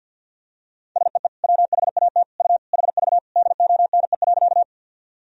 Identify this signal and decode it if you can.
Morse code, heard as a keyed tone, about 52 wpm, 700 Hz. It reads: SEE WHAT U HV DONE.
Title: Sentences from Top 500 Words and Abbreviations 50wpm